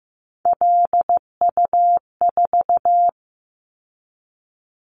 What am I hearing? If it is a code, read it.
Morse code, heard as a keyed tone, about 16 wpm, 700 Hz. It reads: LU4
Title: Callsign Prefixes for Top 50 DXCC - 15wpm